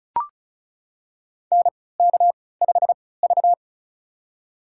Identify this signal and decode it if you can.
Morse code, heard as a keyed tone, about 36 wpm, spaced as for 20 wpm, 700 Hz. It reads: NK5V
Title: USA Call Signs - 3x Character Spacing - 35wpm